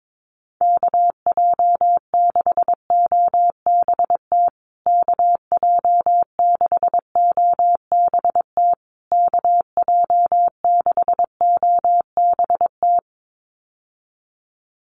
Morse code, heard as a keyed tone, about 22 wpm, 700 Hz. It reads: KJ6OBT KJ6OBT KJ6OBT
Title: USA Call Signs Repeated 3x 22wpm